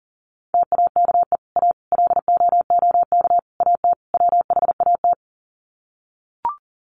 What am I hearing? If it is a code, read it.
Morse code, heard as a keyed tone, about 40 wpm, 700 Hz. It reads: TAKE A LOOK AT WHAT
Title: Sets of 5 Words Encore2 40wpm